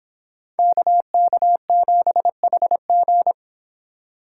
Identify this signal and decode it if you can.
Morse code, heard as a keyed tone, about 26 wpm, 700 Hz. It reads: KK7HG